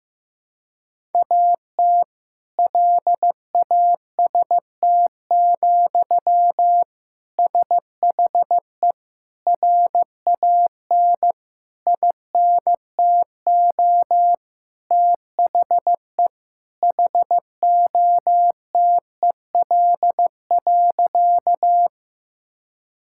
Morse code, heard as a keyed tone, about 16 wpm, 700 Hz. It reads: AT LAST, SHE RAN INTO THE HOTEL.
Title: Sentences from Top 1600 Words 15wpm